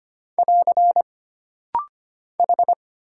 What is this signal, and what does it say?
Morse code, heard as a keyed tone, about 25 wpm, 700 Hz. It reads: <AR> H
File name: Single Character (Letter:Number:Punctuation:Prosign) - Rapid-Fire - 25wpm